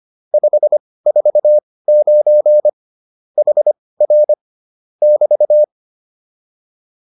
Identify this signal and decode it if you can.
Morse code, heard as a keyed tone, about 25 wpm, spaced as for 18 wpm, 600 Hz. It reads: 549 HR <BT>